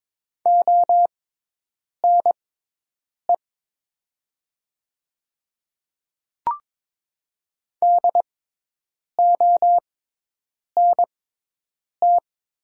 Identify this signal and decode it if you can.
Morse code, heard as a keyed tone, about 22 wpm, spaced as for 8 wpm, 700 Hz. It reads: ONE DONT
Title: 6x Character Spacing - Top 300 Words - 22wpm